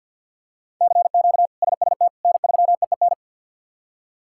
Morse code, heard as a keyed tone, about 52 wpm, 690 Hz. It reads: CQ SST N3EEN
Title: K1USN SST QSO Practice - 50wpm